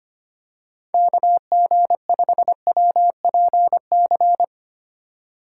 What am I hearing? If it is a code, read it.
Morse code, heard as a keyed tone, about 25 wpm, 700 Hz. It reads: KG5WPC
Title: USA Call Signs - 4x Character Spacing - 25wpm